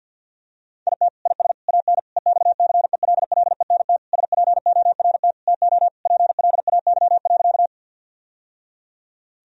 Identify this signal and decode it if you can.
Morse code, heard as a keyed tone, about 50 wpm, 700 Hz. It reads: IT IS AN EXCELLENT SPORT TO PLAY.
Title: Sentences from Top 1000 Words - Review Part 4 - 50wpm